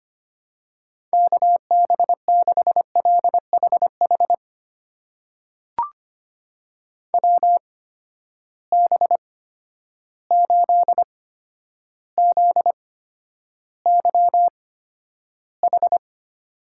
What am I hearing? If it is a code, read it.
Morse code, heard as a keyed tone, about 25 wpm, spaced as for 11 wpm, 700 Hz. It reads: KB6LHH WB8ZYH
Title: USA Call Signs - 6 Characters - 8x Character Spacing - 25wpm